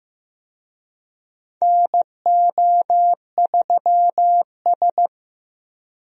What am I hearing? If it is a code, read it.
Morse code, heard as a keyed tone, about 15 wpm, 700 Hz. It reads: NO3S
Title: USA Call Signs - 4 Characters - Repeated 3x - 15wpm